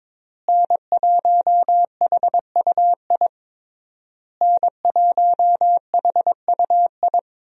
Morse code, heard as a keyed tone, about 22 wpm, 700 Hz. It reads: N1HUI N1HUI
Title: USA Call Signs - 5 Characters - Repeated 3x - 22wpm